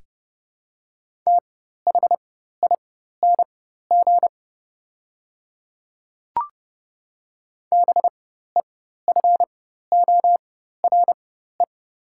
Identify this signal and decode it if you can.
Morse code, heard as a keyed tone, about 30 wpm, spaced as for 14 wpm, 700 Hz. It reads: THING BEFORE
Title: 4x Character Spacing - Top 200 Words - 30wpm